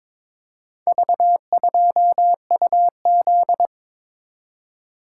Morse code, heard as a keyed tone, about 22 wpm, 700 Hz. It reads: V2UZ
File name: practice / 22wpm / Four Letter-Number Combinations - 22wpm